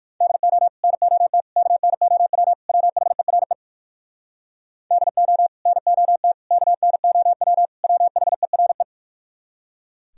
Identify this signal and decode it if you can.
Morse code, heard as a keyed tone, about 49 wpm, 700 Hz. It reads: DO NOT KNOW WHERE DO NOT KNOW WHERE
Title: Speed-Racing - Sets of 4 Words 45wpm